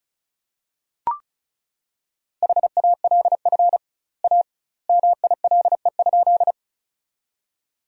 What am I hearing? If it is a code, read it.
Morse code, heard as a keyed tone, about 36 wpm, 700 Hz. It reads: HALF A MILE?
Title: 2x Word-Spacing - Sentences from Top 400 Words - 35wpm